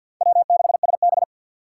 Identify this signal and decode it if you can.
Morse code, heard as a keyed tone, about 48 wpm, 700 Hz. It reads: W6SB
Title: USA Call Signs - 4 Characters - 2x Character Spacing - 50wpm